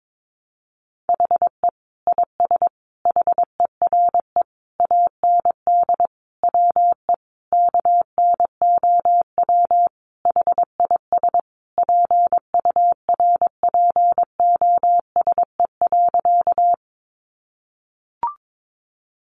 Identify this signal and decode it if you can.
Morse code, heard as a keyed tone, about 22 wpm, 700 Hz. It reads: HE IS HERE AND WE KNOW HIS PURPOSE.
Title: Sentences from Top 800 Words 22wpm